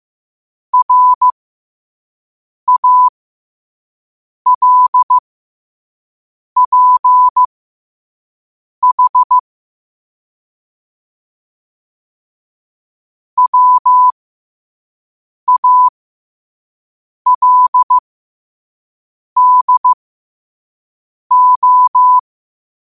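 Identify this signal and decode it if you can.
Morse code, heard as a keyed tone, about 15 wpm, spaced as for 5 wpm, 1000 Hz. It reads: RALPH WALDO
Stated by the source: Morse code at 5 wpm.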